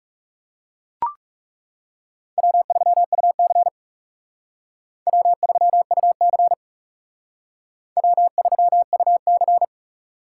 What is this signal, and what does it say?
Morse code, heard as a keyed tone, about 37 wpm, 700 Hz. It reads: W3UC W3UC W3UC
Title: Speed Racing - USA Call Signs 30wpm